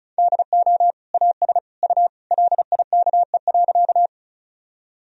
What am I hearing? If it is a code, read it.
Morse code, heard as a keyed tone, about 34 wpm, 700 Hz. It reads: DO AS U LIKE.
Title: Sentences from Top 100 Words and Abbreviations 35wpm